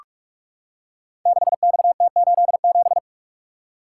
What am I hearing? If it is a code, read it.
Morse code, heard as a keyed tone, about 44 wpm, 690 Hz. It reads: 6XT87
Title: Code Groups - 5 Letter:Numbers - 45wpm